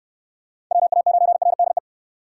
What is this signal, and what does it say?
Morse code, heard as a keyed tone, about 69 wpm, 700 Hz. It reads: WA9RDE